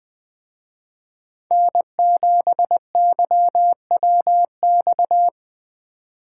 Morse code, heard as a keyed tone, about 21 wpm, 700 Hz. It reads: N7YWX
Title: Speed Racing - USA 1x3 Call Signs 17wpm